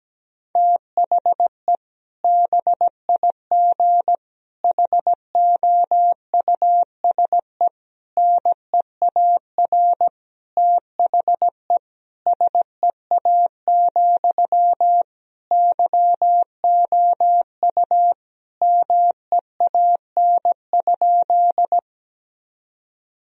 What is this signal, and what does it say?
Morse code, heard as a keyed tone, about 17 wpm, 700 Hz. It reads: THE BIG HOUSE NEAR THE SEA, YOU MEAN?
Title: Sentences from Top 1000 Words - Review Part 1 - 17wpm